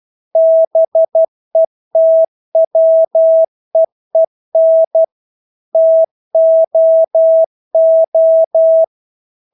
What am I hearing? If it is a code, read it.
Morse code, heard as a keyed tone, about 12 wpm, 650 Hz. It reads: BETWEEN TOO
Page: Engelske ord 12 wpm